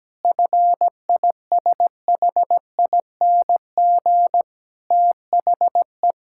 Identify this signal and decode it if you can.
Morse code, heard as a keyed tone, about 17 wpm, 700 Hz. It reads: FISHING THE
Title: Sentences from Top 1000 Words - Review Part 5 - 17wpm